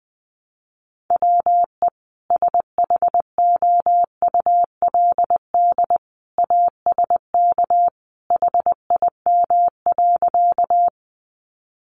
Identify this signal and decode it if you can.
Morse code, heard as a keyed tone, about 20 wpm, 700 Hz. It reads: WE SHOULD ASK HIM.
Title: Sentences from Top 300 Words 20wpm